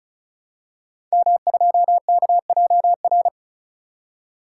Morse code, heard as a keyed tone, about 36 wpm, 700 Hz. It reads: M2KJR
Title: Code Groups - 5 Letter:Numbers - 35wpm